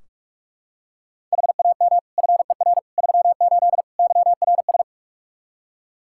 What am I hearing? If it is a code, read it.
Morse code, heard as a keyed tone, about 41 wpm, 700 Hz. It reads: HAM FER 38 YRS
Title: K1USN SST QSO Practice 45wpm